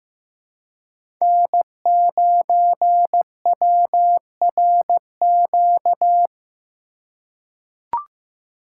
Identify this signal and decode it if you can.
Morse code, heard as a keyed tone, about 16 wpm, 700 Hz. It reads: N9WRQ
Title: USA Call Signs - 5 Characters - 3x Character Spacing - 15wpm